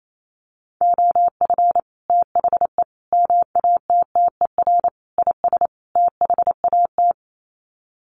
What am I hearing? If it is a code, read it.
Morse code, heard as a keyed tone, about 28 wpm, 700 Hz. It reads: OF THE MATTER IS THAT
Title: Sets of 5 Words Encore 28wpm